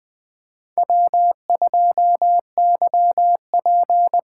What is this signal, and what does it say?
Morse code, heard as a keyed tone, about 20 wpm, 700 Hz. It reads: W2YP
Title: USA Call Signs - 5 Characters - 8x Character Spacing - 20wpm